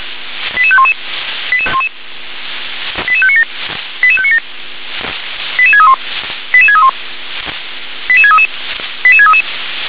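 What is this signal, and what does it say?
Five-tone selective call (ZVEI), heard as quick runs of five tones, tones of about 70 ms. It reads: 88511 88597 88597 88631 88631 88633 88633
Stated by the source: Received by SDR, ZVEI five-tone call 88510.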